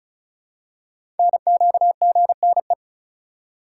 Morse code, heard as a keyed tone, about 36 wpm, 700 Hz. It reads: NQGNE
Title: Code Groups - 5 Letters - 35wpm